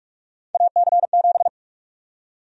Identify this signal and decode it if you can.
Morse code, heard as a keyed tone, about 46 wpm, 700 Hz. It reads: AC7
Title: USA Call Sign Prefixes - 45wpm